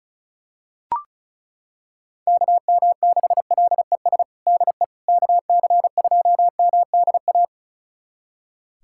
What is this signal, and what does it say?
Morse code, heard as a keyed tone, about 34 wpm, 700 Hz. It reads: KM6LES DE KC2MDA
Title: USA Callsign DE Callsign - 6 Characters - 35wpm